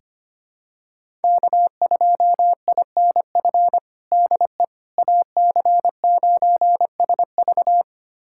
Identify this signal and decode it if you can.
Morse code, heard as a keyed tone, about 25 wpm, 700 Hz. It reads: K2INF DE AC9SV
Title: USA Callsign DE Callsign - 5 Characters - 25wpm